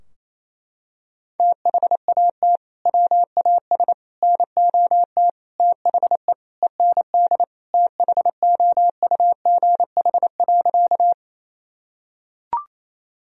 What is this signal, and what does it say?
Morse code, heard as a keyed tone, about 28 wpm, 700 Hz. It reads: THAT WAS NOT THE END THOUGH.
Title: Sentences from Top 1000 Words - Review Part 2 - 28wpm